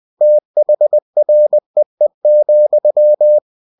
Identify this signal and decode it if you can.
Morse code, heard as a keyed tone, about 20 wpm, 600 Hz. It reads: THREE,
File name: cw-books / winnie-the-pooh / winnie-20wpm-page-0023